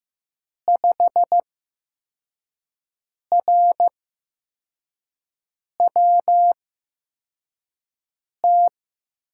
Morse code, heard as a keyed tone, about 15 wpm, spaced as for 4 wpm, 700 Hz. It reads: 5RWT
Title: USA Call Signs - 5 Characters - 8x Character Spacing - 15wpm